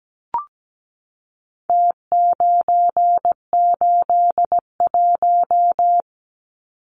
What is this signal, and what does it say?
Morse code, heard as a keyed tone, about 17 wpm, 700 Hz. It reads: T981
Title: Numbers - Contest Style - 4-Digit 17wpm